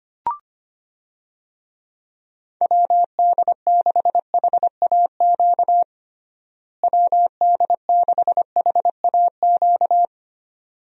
Morse code, heard as a keyed tone, about 25 wpm, 700 Hz. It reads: WD6HAQ WD6HAQ
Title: USA Call Signs - 6 Characters - Repeated 2x - 25wpm